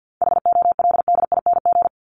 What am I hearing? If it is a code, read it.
Morse code, heard as a keyed tone, about 50 wpm, 700 Hz. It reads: HOLDING
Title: ING Suffix - Top 100 Words with Base Words 50wpm